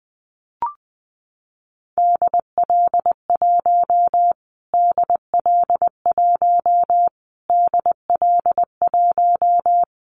Morse code, heard as a keyed tone, about 19 wpm, 700 Hz. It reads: DL1 DL1 DL1
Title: Callsign Prefixes for Top 25 DXCC - Repeated 3x - 20wpm